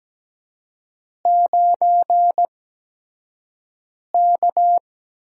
Morse code, heard as a keyed tone, about 17 wpm, spaced as for 5 wpm, 700 Hz. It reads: 9K
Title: USA Call Signs - 4 Characters - 8x Character Spacing - 17wpm